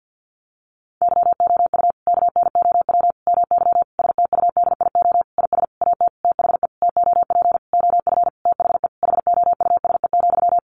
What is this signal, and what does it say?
Morse code, heard as a keyed tone, about 50 wpm, 700 Hz. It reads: YOU KNOW MY STUDIO IS AT THE TOP OF THE HOUSE,
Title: Sentences from Top 1200 Words 50wpm